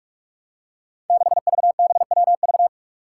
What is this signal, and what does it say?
Morse code, heard as a keyed tone, about 46 wpm, 700 Hz. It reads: 6VBWV
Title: Five Challenging Letter-Number Combinations 45wpm